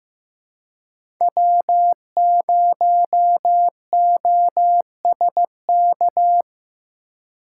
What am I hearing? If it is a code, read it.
Morse code, heard as a keyed tone, about 15 wpm, 700 Hz. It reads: W0OSK
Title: USA Call Signs - 5 Characters - 4x Character Spacing - 15wpm